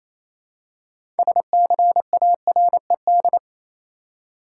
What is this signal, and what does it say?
Morse code, heard as a keyed tone, about 28 wpm, 700 Hz. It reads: SCARED
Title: Six Letter Words - 28wpm